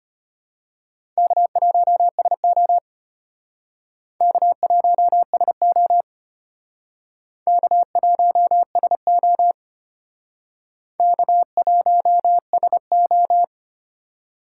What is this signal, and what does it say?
Morse code, heard as a keyed tone, about 32 wpm, 700 Hz. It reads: K1SO K1SO K1SO K1SO